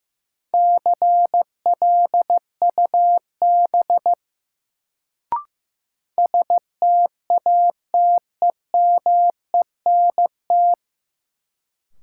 Morse code, heard as a keyed tone, about 15 wpm, 700 Hz. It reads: CLUB STATEMENT
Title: Sentences from Top 1200 Words - New Word Drill - 15wpm